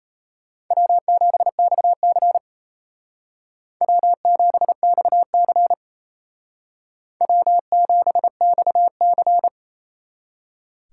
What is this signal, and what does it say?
Morse code, heard as a keyed tone, about 35 wpm, 700 Hz. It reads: W7XC W7XC W7XC
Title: Speed Racing - USA 1x2 Call Signs 28wpm